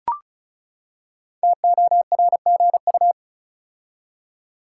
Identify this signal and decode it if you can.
Morse code, heard as a keyed tone, about 36 wpm, 700 Hz. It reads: TORGU
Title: Code Groups - 5 Letters - 35wpm